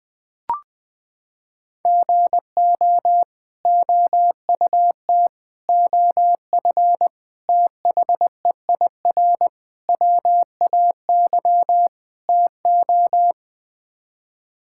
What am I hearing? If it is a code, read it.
Morse code, heard as a keyed tone, about 20 wpm, 700 Hz. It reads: GO OUT OF THEIR WAY TO
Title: Sets of 6 Words - 20wpm